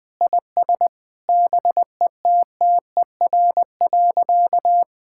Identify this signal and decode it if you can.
Morse code, heard as a keyed tone, about 20 wpm, 700 Hz. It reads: IS BETTER.
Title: Sentences with 2-Word Phrases - Spoken Beginning - 20wpm